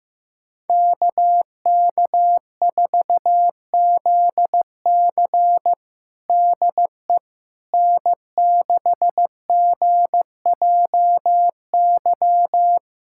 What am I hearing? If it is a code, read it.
Morse code, heard as a keyed tone, about 15 wpm, 700 Hz. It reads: KK4ZC DE N6GJY
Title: USA Callsign DE Callsign - 5 Characters - 15wpm